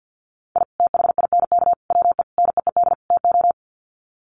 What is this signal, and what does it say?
Morse code, heard as a keyed tone, about 50 wpm, 700 Hz. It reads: I THINK WE NEED TO
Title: Sets of 5 Words 50wpm